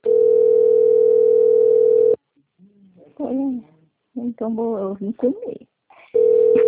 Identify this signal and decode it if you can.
Ringback tone (North America), 440 + 480 Hz, with speech.
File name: rg-1600-unknown- - - .2258